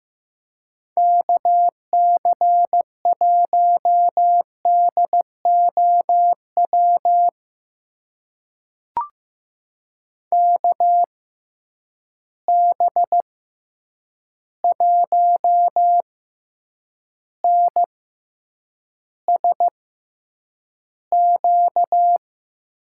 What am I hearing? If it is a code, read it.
Morse code, heard as a keyed tone, about 15 wpm, spaced as for 8 wpm, 700 Hz. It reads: KC1DOW KB1NSQ
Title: USA Call Signs - 6 Characters - 6x Character Spacing - 15wpm